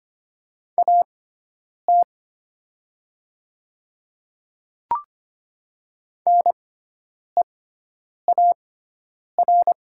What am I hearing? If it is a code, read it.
Morse code, heard as a keyed tone, about 25 wpm, spaced as for 9 wpm, 700 Hz. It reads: AT NEAR
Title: 6x Character Spacing - Top 200 Words - 25wpm